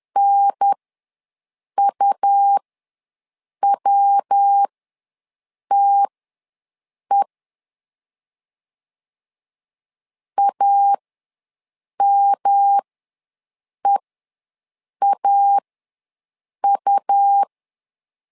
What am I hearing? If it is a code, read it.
Morse code, heard as a keyed tone, about 10 wpm, spaced as for 6 wpm, 800 Hz. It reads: NUWTE AMEAU
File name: LEZIONI / 3 MWRO / 3 MW ws 10wpm 2213